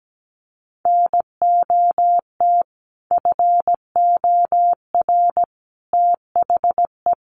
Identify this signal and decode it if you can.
Morse code, heard as a keyed tone, about 17 wpm, 700 Hz. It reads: NOT FOR THE
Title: Sentences from Top 600 Words 17wpm